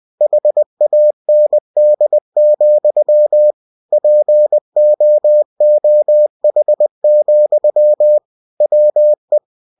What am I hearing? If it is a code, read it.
Morse code, heard as a keyed tone, about 20 wpm, 600 Hz. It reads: HAND, POOH, WE
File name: cw-books / winnie-the-pooh / winnie-20wpm-page-0034